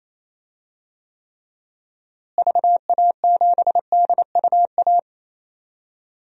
Morse code, heard as a keyed tone, about 28 wpm, 700 Hz. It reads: VA7DUA
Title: DXCC-100 Call Signs - 6 Characters - Repeated 2x - 28wpm